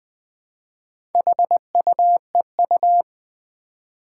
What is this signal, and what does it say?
Morse code, heard as a keyed tone, about 20 wpm, 700 Hz. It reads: HUEU
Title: Four Challenging Letter-Number Combinations 20wpm